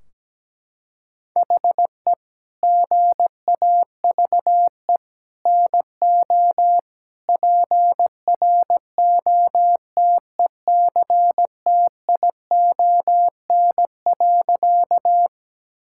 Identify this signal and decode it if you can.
Morse code, heard as a keyed tone, about 17 wpm, 700 Hz. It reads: HE GAVE NO PROTECTION.
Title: Sentences from Top 1600 Words 17wpm